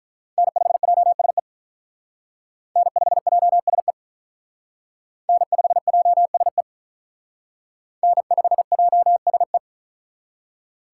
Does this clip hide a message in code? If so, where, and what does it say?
Yes: Morse code from 0.4 to 9.6 s, heard as a keyed tone, about 45 wpm, 700 Hz. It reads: N5JSE N5JSE N5JSE N5JSE